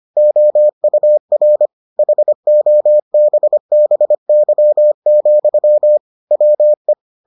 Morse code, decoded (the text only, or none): OUR HOBBY, WE